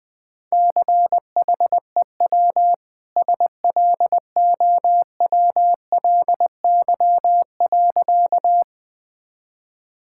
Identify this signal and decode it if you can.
Morse code, heard as a keyed tone, about 21 wpm, 700 Hz. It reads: CHEW SLOWLY.
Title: Easy Sentences with Hints - 20wpm